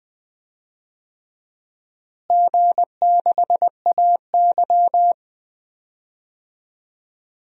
Morse code, heard as a keyed tone, about 20 wpm, 700 Hz. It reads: G6AY
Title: DXCC-100 Call Signs - 4 Characters - Repeated 3x - 20wpm